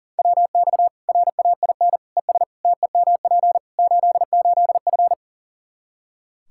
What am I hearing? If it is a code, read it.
Morse code, heard as a keyed tone, about 40 wpm, 700 Hz. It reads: WX RAIN ES TEMP 88F
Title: Practice QSO - Exchange 2 - Info - 40wpm